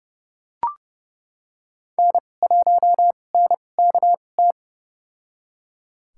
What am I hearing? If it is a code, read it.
Morse code, heard as a keyed tone, about 30 wpm, spaced as for 22 wpm, 700 Hz. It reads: N1NKT
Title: USA Call Signs - 5 Characters - 2x Character Spacing - 30wpm